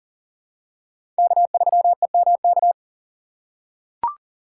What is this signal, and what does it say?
Morse code, heard as a keyed tone, about 40 wpm, 700 Hz. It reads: K3EMK
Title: USA Call Signs - 5 Characters - 3x Character Spacing - 40wpm